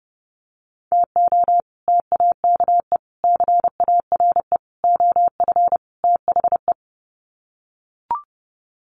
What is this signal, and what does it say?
Morse code, heard as a keyed tone, about 30 wpm, 700 Hz. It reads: TO TAKE CARE OF THE
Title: Sets of 5 Words Encore 30wpm